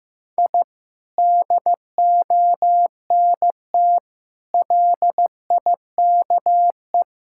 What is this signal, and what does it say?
Morse code, heard as a keyed tone, about 15 wpm, 700 Hz. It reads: I DONT LIKE
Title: Sentences from Top 1000 Words - Review Part 6 - 15wpm